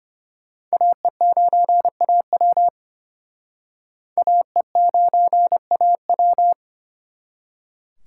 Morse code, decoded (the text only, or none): AE9AW AE9AW